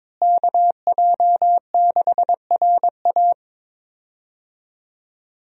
Morse code, heard as a keyed tone, about 22 wpm, 700 Hz. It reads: KJ6RA
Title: USA Call Signs - 5 Characters - Repeated 3x - 22wpm